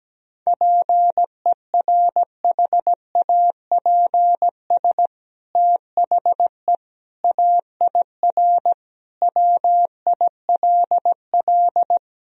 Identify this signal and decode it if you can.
Morse code, heard as a keyed tone, about 17 wpm, 700 Hz. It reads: PERHAPS THE AIR WILL
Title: Sentences from Top 1000 Words - Review Part 1 - 17wpm